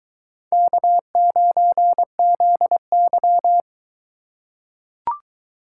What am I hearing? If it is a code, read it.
Morse code, heard as a keyed tone, about 21 wpm, 700 Hz. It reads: K9ZY